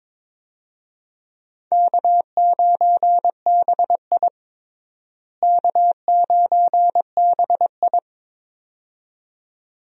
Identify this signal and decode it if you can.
Morse code, heard as a keyed tone, about 22 wpm, 700 Hz. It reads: K9BI K9BI
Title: USA Call Signs - 4 Characters - Repeated 2x - 22wpm